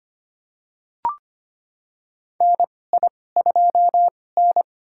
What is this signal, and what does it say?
Morse code, heard as a keyed tone, about 25 wpm, spaced as for 18 wpm, 700 Hz. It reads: NI2N
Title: USA Call Signs - 2x Character Spacing - 25wpm